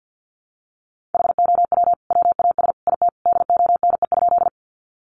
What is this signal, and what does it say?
Morse code, heard as a keyed tone, about 50 wpm, 700 Hz. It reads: HOW WAS IT DONE?
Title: Sentences from Top 500 Words 50wpm